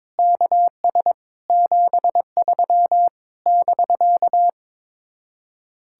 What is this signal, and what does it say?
Morse code, heard as a keyed tone, about 22 wpm, 700 Hz. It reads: KS 73 <BK>
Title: Speed-Racing Common QSO Phrases - 22wpm